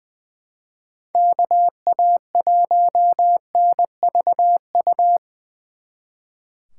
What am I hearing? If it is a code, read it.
Morse code, heard as a keyed tone, about 24 wpm, 700 Hz. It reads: KA1NVU
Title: Speed Racing - USA 2x3 Call Signs 20wpm